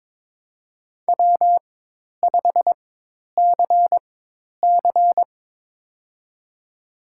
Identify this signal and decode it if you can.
Morse code, heard as a keyed tone, about 22 wpm, spaced as for 10 wpm, 700 Hz. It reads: W5CC
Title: USA Call Signs - 4 Characters - 4x Character Spacing - 22wpm